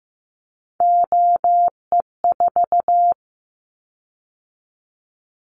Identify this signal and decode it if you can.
Morse code, heard as a keyed tone, about 15 wpm, 700 Hz. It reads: OE4
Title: Callsign Prefixes for Top 100 DXCC - 15wpm